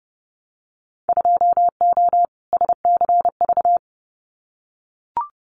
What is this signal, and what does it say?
Morse code, heard as a keyed tone, about 30 wpm, 700 Hz. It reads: 2O SCV
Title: Winter Field Day Exchanges Encore 30wpm